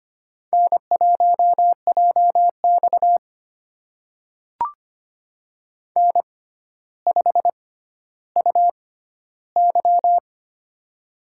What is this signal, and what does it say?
Morse code, heard as a keyed tone, about 25 wpm, spaced as for 13 wpm, 700 Hz. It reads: N1JX N5UY
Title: USA Call Signs - 4 Characters - 6x Character Spacing - 25wpm